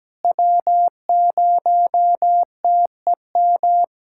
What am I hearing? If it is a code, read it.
Morse code, heard as a keyed tone, about 17 wpm, 700 Hz. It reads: W0TEM
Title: SOTA QSO Practice 17wpm